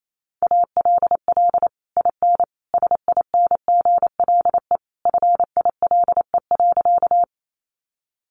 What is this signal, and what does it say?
Morse code, heard as a keyed tone, about 28 wpm, 700 Hz. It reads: ALL IN SINGLE FILE.
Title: Sentences from Top 1200 Words 28wpm